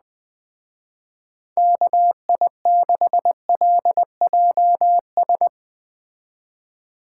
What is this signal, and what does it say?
Morse code, heard as a keyed tone, about 21 wpm, 700 Hz. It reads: KI6LJS